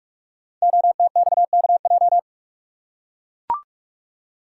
Morse code, heard as a keyed tone, about 44 wpm, 700 Hz. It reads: OTXKJ
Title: Code Groups - 5 Letter:Numbers - 45wpm